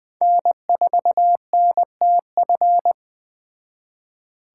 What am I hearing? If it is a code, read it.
Morse code, heard as a keyed tone, about 21 wpm, 700 Hz. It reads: N4NTF